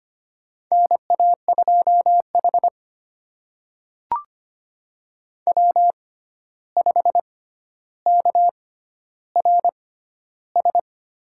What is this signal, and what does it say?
Morse code, heard as a keyed tone, about 25 wpm, spaced as for 12 wpm, 700 Hz. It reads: NA2H W5KRS
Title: USA Call Signs - 6x Character Spacing - 25wpm